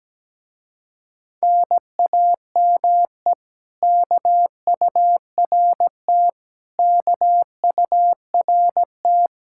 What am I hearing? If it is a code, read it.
Morse code, heard as a keyed tone, about 17 wpm, 700 Hz. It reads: NAME KURT KURT
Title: Speed-Racing Common QSO Phrases - 17wpm